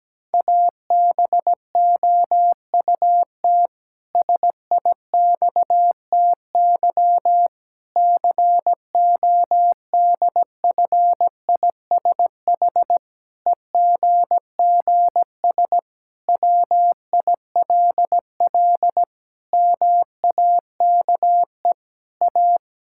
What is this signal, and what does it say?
Morse code, heard as a keyed tone, about 17 wpm, 700 Hz. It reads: ABOUT SIXTY CODFISH EGGS WILL MAKE A
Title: Pangram Challenge v2 - 17wpm